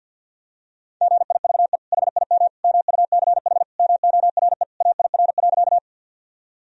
Morse code, heard as a keyed tone, about 51 wpm, 700 Hz. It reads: GIVE HIM MUCH MORE AIR.
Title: Sentences from Top 1000 Words - Review Part 1 - 50wpm